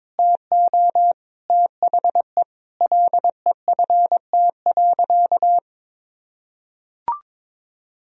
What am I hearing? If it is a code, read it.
Morse code, heard as a keyed tone, about 22 wpm, 700 Hz. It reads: TO THE LEFT.
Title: Sentences from Top 1000 Words - Review Part 2 - 22wpm